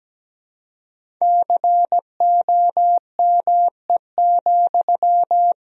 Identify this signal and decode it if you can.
Morse code, heard as a keyed tone, about 17 wpm, 700 Hz. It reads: COME,
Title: Sentences from Top 500 Words and Abbreviations 17wpm